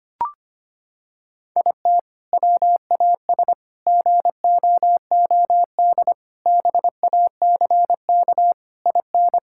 Morse code, heard as a keyed tone, about 25 wpm, 700 Hz. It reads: IT WAS GOOD BACK IN
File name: practice / 25wpm / Sentences with 3-Word Phrases - 25wpm